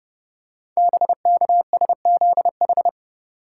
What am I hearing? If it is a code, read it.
Morse code, heard as a keyed tone, about 30 wpm, 700 Hz. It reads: BKSZH
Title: Code Groups - 5 Letters - 30wpm